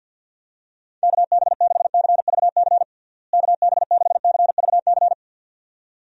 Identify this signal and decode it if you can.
Morse code, heard as a keyed tone, about 51 wpm, 700 Hz. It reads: KB6CVC KB6CVC
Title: USA Call Signs - 6 Characters - Repeated 2x - 50wpm